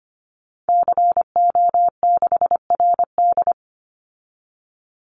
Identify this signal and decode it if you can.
Morse code, heard as a keyed tone, about 25 wpm, 700 Hz. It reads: CO6RD
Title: Call Signs - 25wpm